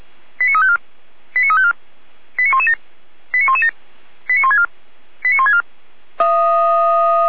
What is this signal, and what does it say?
Five-tone selective call (ZVEI), heard as quick runs of five tones, tones of about 70 ms. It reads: 89354 89354 89117 89117 89164 89164